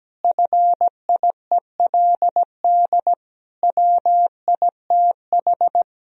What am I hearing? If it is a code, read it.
Morse code, heard as a keyed tone, about 17 wpm, 700 Hz. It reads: FIELD WITH
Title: Sentences from Top 1000 Words - Review Part 2 - 17wpm